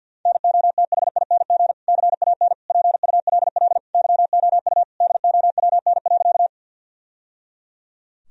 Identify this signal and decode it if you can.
Morse code, heard as a keyed tone, about 50 wpm, 700 Hz. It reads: NOTHING CAN PULL YOU DOWN.
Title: Sentences from Top 1000 Words - Review Part 2 - 50wpm